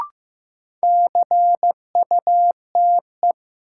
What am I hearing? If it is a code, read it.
Morse code, heard as a keyed tone, about 15 wpm, 700 Hz. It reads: CUTE
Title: Sentences from Top 1000 Words - Review Part 7 - New Word Drill - 15wpm